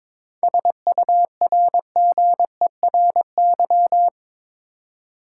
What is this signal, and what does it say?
Morse code, heard as a keyed tone, about 22 wpm, 700 Hz. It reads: SURGERY
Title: Seven Letter Words - 22wpm